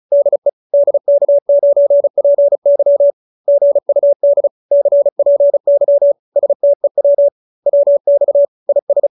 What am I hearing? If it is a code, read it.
Morse code, heard as a keyed tone, about 34 wpm, 570 Hz. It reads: DE DK9PY GUD CPY STEW WX IS